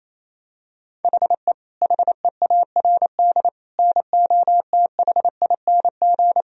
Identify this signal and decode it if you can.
Morse code, heard as a keyed tone, about 28 wpm, 700 Hz. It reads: HE HEARD NOTHING
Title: Sentences from Top 1000 Words - Review Part 3 - 28wpm